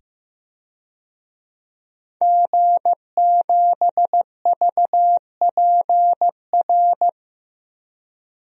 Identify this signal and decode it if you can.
Morse code, heard as a keyed tone, about 15 wpm, 700 Hz. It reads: G7VPR
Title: DXCC-100 Call Signs - 5 Characters - 15wpm